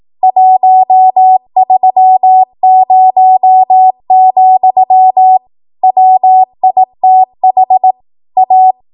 Morse code, heard as a keyed tone, about 18 wpm, 750 Hz. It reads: 130, WITH A
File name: Morse / 18WPM